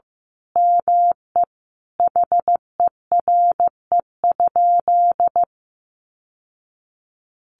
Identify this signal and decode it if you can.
Morse code, heard as a keyed tone, about 14 wpm, 700 Hz. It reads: ME HERE?
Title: Sentences from Top 1200 Words 15wpm